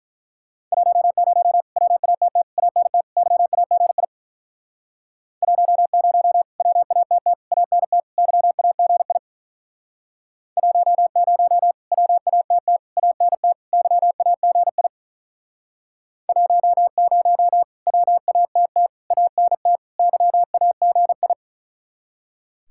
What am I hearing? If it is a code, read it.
Morse code, heard as a keyed tone, about 46 wpm, 700 Hz. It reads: 10 WATT ANT YAGI 10 WATT ANT YAGI 10 WATT ANT YAGI 10 WATT ANT YAGI